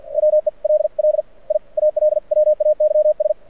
Morse code, heard as a keyed tone, about 51 wpm, spaced as for 44 wpm, 590 Hz. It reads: WELL I ALWAYS
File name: archived / CRSnet